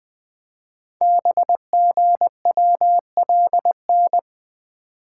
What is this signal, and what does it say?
Morse code, heard as a keyed tone, about 20 wpm, 700 Hz. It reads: BGWLN
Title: Code Groups - 5 Letters - 20wpm